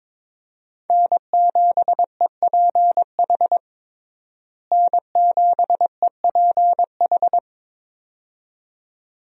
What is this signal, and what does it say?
Morse code, heard as a keyed tone, about 22 wpm, 700 Hz. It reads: N7EPH N7EPH
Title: USA Call Signs - 5 Characters - Repeated 2x - 22wpm